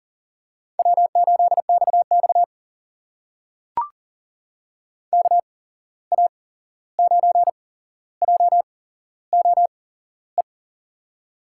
Morse code, heard as a keyed tone, about 40 wpm, spaced as for 15 wpm, 690 Hz. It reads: W8XX KA9JOE
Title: USA Call Signs - 8x Character Spacing - 40wpm